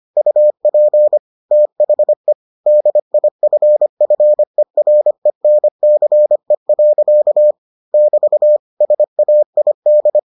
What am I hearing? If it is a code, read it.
Morse code, heard as a keyed tone, about 25 wpm, 600 Hz. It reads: UP THE DIFFERENCE. <BT> SAID